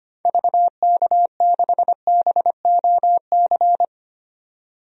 Morse code, heard as a keyed tone, about 25 wpm, 700 Hz. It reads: VK6BOC
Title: DXCC-100 Call Signs - 6 Characters - 25wpm